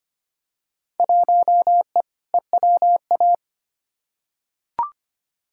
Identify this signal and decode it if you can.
Morse code, heard as a keyed tone, about 25 wpm, 700 Hz. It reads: 1E EWA
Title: Field Day Exchanges - 25wpm